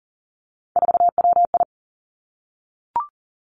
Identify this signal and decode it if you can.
Morse code, heard as a keyed tone, about 42 wpm, 700 Hz. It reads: 4WI
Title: Three Challenging Letter-Number Combinations 40wpm